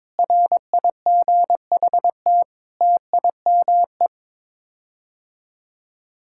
Morse code, heard as a keyed tone, about 22 wpm, 700 Hz. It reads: RIGHT TIME
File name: Sets of 9 Words - 22wpm